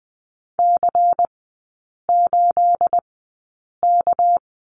Morse code, heard as a keyed tone, about 20 wpm, spaced as for 8 wpm, 690 Hz. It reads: C8K
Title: 212 Lesson 34e - Number 8 - Three Characters 20wpm